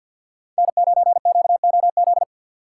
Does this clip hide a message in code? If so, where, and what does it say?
Yes: Morse code from 0.6 to 2.2 s, heard as a keyed tone, about 64 wpm, 700 Hz. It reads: N9QOZ